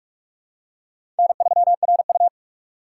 Morse code, heard as a keyed tone, about 46 wpm, 700 Hz. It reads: N3RU
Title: USA Call Signs - 4 Characters - 4x Character Spacing - 45wpm